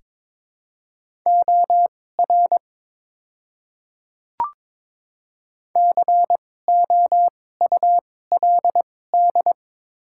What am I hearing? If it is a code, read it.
Morse code, heard as a keyed tone, about 23 wpm, spaced as for 16 wpm, 700 Hz. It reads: OR COULD
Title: 2x Character Spacing - Top 100 Words - 22wpm